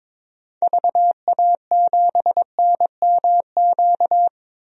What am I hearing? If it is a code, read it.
Morse code, heard as a keyed tone, about 22 wpm, 700 Hz. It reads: VA7NMQ
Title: DXCC-100 Call Signs - 6 Characters - 22wpm